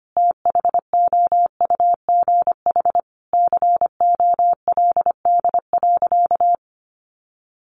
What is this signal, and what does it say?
Morse code, heard as a keyed tone, about 25 wpm, 700 Hz. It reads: THOUGH COLD.
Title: Sentences from Top 1000 Words 25wpm